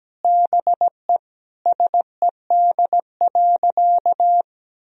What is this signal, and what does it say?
Morse code, heard as a keyed tone, about 17 wpm, 700 Hz. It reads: BE SED.
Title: Sentences from Top 100 Words and Abbreviations 17wpm